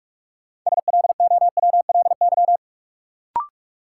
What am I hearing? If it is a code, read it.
Morse code, heard as a keyed tone, about 42 wpm, 700 Hz. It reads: SLOWLY
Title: Sentences from Top 1100 Words - New Word Drill - 45wpm